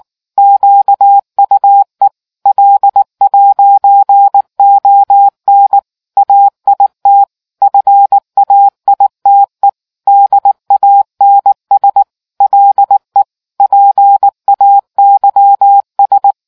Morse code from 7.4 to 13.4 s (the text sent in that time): FAITE DANS LE